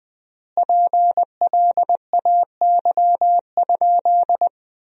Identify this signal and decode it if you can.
Morse code, heard as a keyed tone, about 20 wpm, 700 Hz. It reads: PLAY?
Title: 3x Word-Spacing - Sentences from Top 400 Words - 20wpm